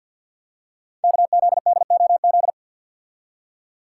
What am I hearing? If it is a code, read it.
Morse code, heard as a keyed tone, about 50 wpm, 700 Hz. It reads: KZDOZ
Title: Code Groups - 5 Letter:Numbers - 50wpm